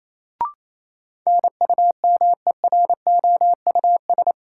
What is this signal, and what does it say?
Morse code, heard as a keyed tone, about 28 wpm, 700 Hz. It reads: NUMEROUS
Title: Sentences from Top 1400 Words - New Word Drill - 28wpm